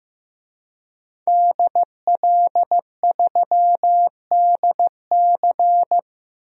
Morse code, heard as a keyed tone, about 16 wpm, 700 Hz. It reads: DL3DC
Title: DXCC-100 Call Signs - 5 Characters - Repeated 3x - 15wpm